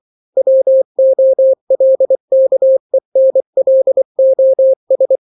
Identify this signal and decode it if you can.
Morse code, heard as a keyed tone, about 24 wpm, 530 Hz. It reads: WOLKENLOS
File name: daily / 23wpm